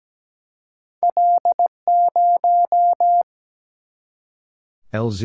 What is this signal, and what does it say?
Morse code, heard as a keyed tone, about 17 wpm, 700 Hz. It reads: L0
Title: Single-Letter Single-Number 17wpm